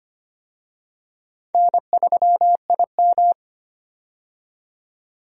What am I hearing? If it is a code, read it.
Morse code, heard as a keyed tone, about 25 wpm, 700 Hz. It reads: N3IM
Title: USA Call Signs - 4 Characters - 25wpm